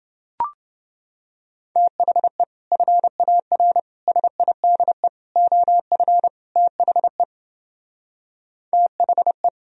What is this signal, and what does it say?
Morse code, heard as a keyed tone, about 29 wpm, 700 Hz. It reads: THE FAR SIDE OF THE THE